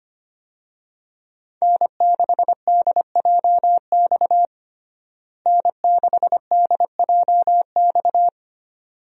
Morse code, heard as a keyed tone, about 25 wpm, 700 Hz. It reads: N6DJX N6DJX
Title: USA Call Signs - 5 Characters - Repeated 2x - 25wpm